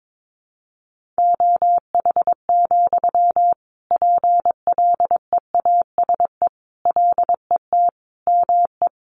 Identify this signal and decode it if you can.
Morse code, heard as a keyed tone, about 22 wpm, 700 Hz. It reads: OH, PLEASE LET ME
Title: Sentences from Top 1500 Words 22wpm